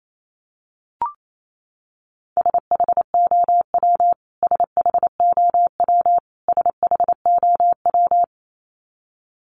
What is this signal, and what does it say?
Morse code, heard as a keyed tone, about 28 wpm, 700 Hz. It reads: SHOW SHOW SHOW
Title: Top 500 Words Repeated 28wpm